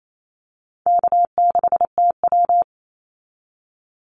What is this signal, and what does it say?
Morse code, heard as a keyed tone, about 28 wpm, 700 Hz. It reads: K6TW
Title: NA SOTA Activator Callsigns 28wpm